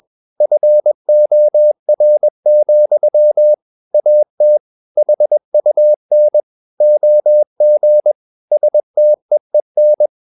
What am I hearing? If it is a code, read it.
Morse code, heard as a keyed tone, about 21 wpm, 600 Hz. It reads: FOR, AT HUN OG STEEN
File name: Ferie QRS 21wpm